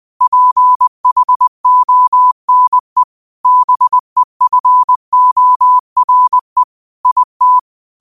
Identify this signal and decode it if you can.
Morse code, heard as a keyed tone, about 20 wpm, 1000 Hz. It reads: PHONE BEFORE IT